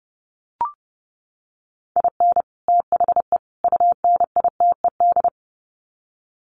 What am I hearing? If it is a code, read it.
Morse code, heard as a keyed tone, about 30 wpm, 700 Hz. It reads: IN THE UNITED
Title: Sets of 3 Words 30wpm